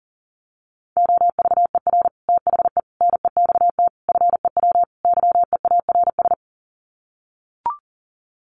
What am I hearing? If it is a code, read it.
Morse code, heard as a keyed tone, about 40 wpm, 700 Hz. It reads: OVER THE NEXT FEW YEARS
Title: Sets of 5 Words Encore 40wpm